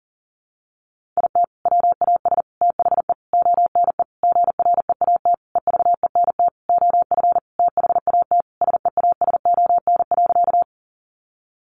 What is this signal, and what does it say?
Morse code, heard as a keyed tone, about 40 wpm, 700 Hz. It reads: IT WAS THE ONE GREAT EVENT OF THAT SEASON.
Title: Sentences from Top 1400 Words 40wpm